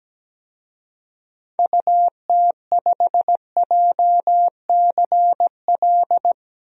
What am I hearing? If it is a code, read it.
Morse code, heard as a keyed tone, about 17 wpm, 700 Hz. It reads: UT5JCL
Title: DXCC-100 Call Signs - 6 Characters - Repeated 2x - 17wpm